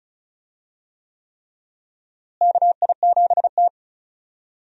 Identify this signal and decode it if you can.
Morse code, heard as a keyed tone, about 36 wpm, 700 Hz. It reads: KI7T
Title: USA Call Signs - 4 Characters - Repeated 3x - 35wpm